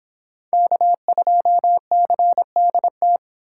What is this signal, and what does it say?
Morse code, heard as a keyed tone, about 30 wpm, 700 Hz. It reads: K2CDT